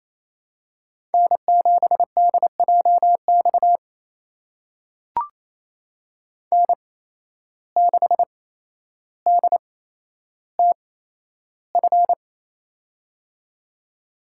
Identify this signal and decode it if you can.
Morse code, heard as a keyed tone, about 28 wpm, spaced as for 12 wpm, 700 Hz. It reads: N7DJX N6DTF
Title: USA Call Signs - 8x Character Spacing - 28wpm